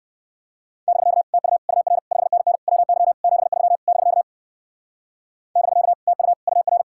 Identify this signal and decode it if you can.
Morse code, heard as a keyed tone, about 59 wpm, 700 Hz. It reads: <BK> TU UR 5NN WY 73 <BK> <BK> TU UR